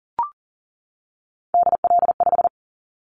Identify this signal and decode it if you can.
Morse code, heard as a keyed tone, about 40 wpm, 700 Hz. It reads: DL5
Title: Callsign Prefixes for Top 50 DXCC - 40wpm